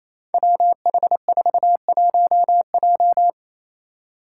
Morse code, heard as a keyed tone, about 28 wpm, 700 Hz. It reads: WH41J
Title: Five Challenging Letter-Number Combinations 28wpm